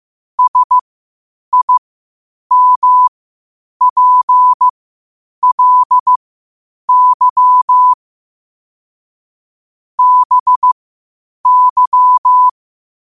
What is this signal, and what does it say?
Morse code, heard as a keyed tone, about 15 wpm, spaced as for 8 wpm, 1000 Hz. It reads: SIMPLY BY
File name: qotdfiles / 8wpm